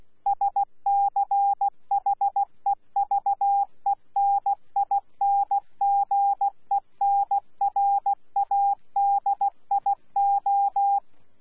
Morse code, heard as a keyed tone, about 16 wpm, 800 Hz. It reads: SCHEVENINGENRADIO
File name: Z PI4VRZ / PI4VRZ 14-jun-2025